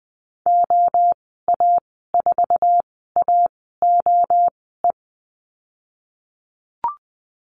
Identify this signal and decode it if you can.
Morse code, heard as a keyed tone, about 20 wpm, spaced as for 14 wpm, 700 Hz. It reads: OA4AOE
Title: 055 Lesson 10g - Letter r - Call Signs 20wpm